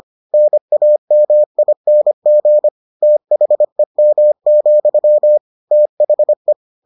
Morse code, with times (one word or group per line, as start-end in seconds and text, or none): 0.3-2.7 NAMING
3.0-5.4 THEM,
5.7-6.5 THE